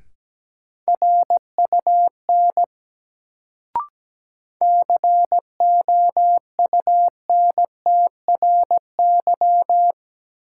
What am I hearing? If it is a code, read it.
Morse code, heard as a keyed tone, about 17 wpm, 700 Hz. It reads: RUN COUNTRY
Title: Sentences from Top 300 Words - New Word Drill - 17wpm